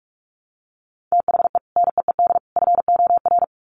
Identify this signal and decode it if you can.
Morse code, heard as a keyed tone, about 44 wpm, 700 Hz. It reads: THE NEED FOR
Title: Sets of 3 Words 45wpm